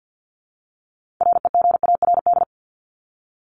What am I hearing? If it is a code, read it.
Morse code, heard as a keyed tone, about 50 wpm, 700 Hz. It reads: REGARD
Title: ED Suffix - Top 100 Words with Base Words 50wpm